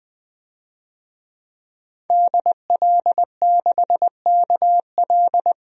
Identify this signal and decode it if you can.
Morse code, heard as a keyed tone, about 20 wpm, 700 Hz. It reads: DL6KL
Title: DXCC-100 Call Signs - 5 Characters - Repeated 3x - 20wpm